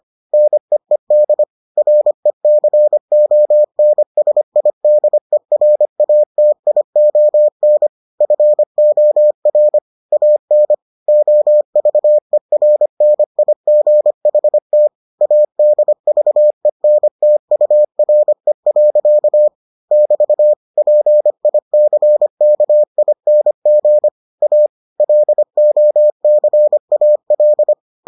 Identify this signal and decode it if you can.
Morse code, heard as a keyed tone, about 25 wpm, 600 Hz. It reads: NEED RECONSIDERATION FOR AN OVERNIGHT ADVENTURE. <BT> PICKING A LOCAL